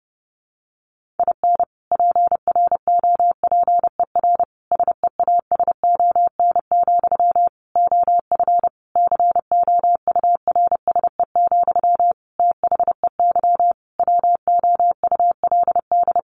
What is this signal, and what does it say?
Morse code, heard as a keyed tone, about 30 wpm, 700 Hz. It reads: IN PROPER SEASON, OF COURSE, THEY WOULD
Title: Sentences from Top 1400 Words 30wpm